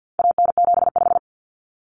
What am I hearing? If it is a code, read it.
Morse code, heard as a keyed tone, about 50 wpm, 700 Hz. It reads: AN75
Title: Common Cut Numbers - 4-Digit 50wpm